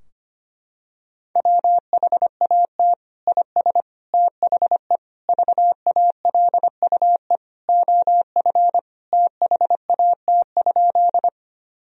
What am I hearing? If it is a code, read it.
Morse code, heard as a keyed tone, about 25 wpm, 700 Hz. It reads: WHAT IS THE VALUE OF THAT?
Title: Sentences from Top 1000 Words - Review Part 3 - 25wpm